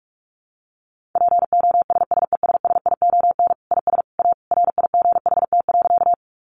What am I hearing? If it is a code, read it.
Morse code, heard as a keyed tone, about 44 wpm, 700 Hz. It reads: POSSESSION IS A RIGHT.
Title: Sentences from Top 1400 Words 45wpm